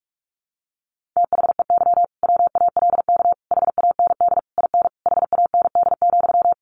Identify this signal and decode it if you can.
Morse code, heard as a keyed tone, about 44 wpm, 700 Hz. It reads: THEY WALK HAND IN HAND,
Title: Sentences from Top 1400 Words 45wpm